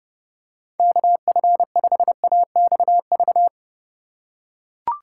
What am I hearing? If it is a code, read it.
Morse code, heard as a keyed tone, about 32 wpm, 700 Hz. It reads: KF5AXV
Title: USA Call Signs - 6 Characters - 4x Character Spacing - 30wpm